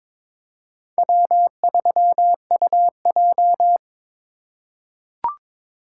Morse code, heard as a keyed tone, about 22 wpm, 700 Hz. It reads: W3UJ
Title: USA Call Signs - 4 Characters - 2x Character Spacing - 22wpm